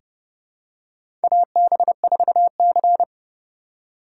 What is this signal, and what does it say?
Morse code, heard as a keyed tone, about 30 wpm, 700 Hz. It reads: AB4C
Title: USA Call Signs - 4x Character Spacing - 30wpm